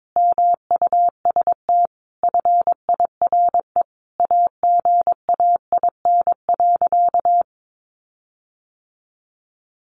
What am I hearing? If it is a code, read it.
Morse code, heard as a keyed tone, about 22 wpm, 700 Hz. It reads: MUST FIRE AGAIN.
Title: Sentences from Top 1400 Words 22wpm